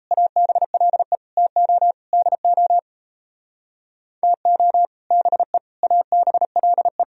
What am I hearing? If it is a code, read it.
Morse code, heard as a keyed tone, about 37 wpm, 700 Hz. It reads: ABLE TO DO TO BE ABLE